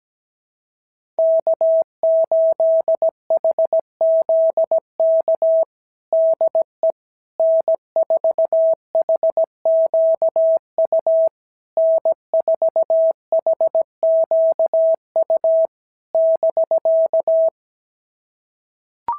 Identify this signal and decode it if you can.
Morse code, heard as a keyed tone, about 15 wpm, 650 Hz. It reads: K8HZK DE N4HQU N4HQU <BK>
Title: SOTA QSO Practice 17wpm